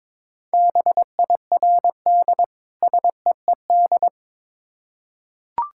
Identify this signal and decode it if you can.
Morse code, heard as a keyed tone, about 22 wpm, 700 Hz. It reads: BIRD SEED
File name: Difficult Words with Hints - 22wpm